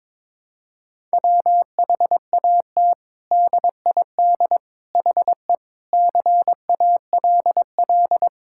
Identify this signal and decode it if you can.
Morse code, heard as a keyed tone, about 22 wpm, 700 Hz. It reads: WHAT DID HE CALL
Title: Sentences from Top 1600 Words 22wpm